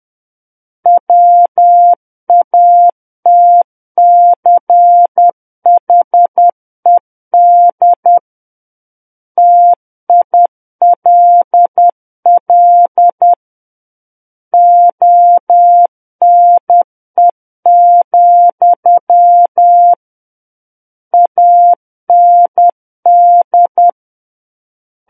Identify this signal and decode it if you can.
Morse code, heard as a keyed tone, about 10 wpm, 700 Hz. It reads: WATCHED TILL ONE, AND
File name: Book One / 01-Chapter 1 - 10 WPM